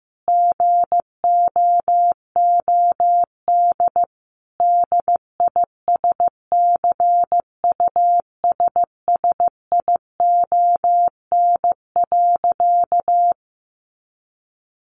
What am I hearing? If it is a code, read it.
Morse code, heard as a keyed tone, about 15 wpm, 700 Hz. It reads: GOOD DISCUSSION.
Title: Sentences from Top 1100 Words 15wpm